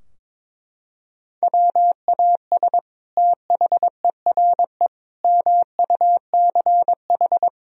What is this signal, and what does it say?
Morse code, heard as a keyed tone, about 22 wpm, 700 Hz. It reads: WAS THERE MUCH
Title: Sentences from Top 1000 Words - Review Part 4 - 22wpm